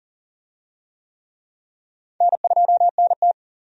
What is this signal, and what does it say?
Morse code, heard as a keyed tone, about 40 wpm, 700 Hz. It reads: N2NT
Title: USA Call Signs - 4 Characters - 40wpm